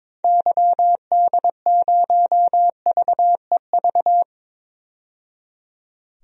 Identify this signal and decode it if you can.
Morse code, heard as a keyed tone, about 22 wpm, 700 Hz. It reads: YD0VEV
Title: DXCC-100 Call Signs - 6 Characters - 22wpm